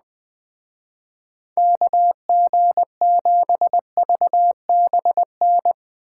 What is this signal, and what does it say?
Morse code, heard as a keyed tone, about 20 wpm, 700 Hz. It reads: KG7VBN